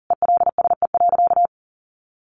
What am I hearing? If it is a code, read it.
Morse code, heard as a keyed tone, about 40 wpm, 700 Hz. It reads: ELSE.
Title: Sentences from Top 800 Words 40wpm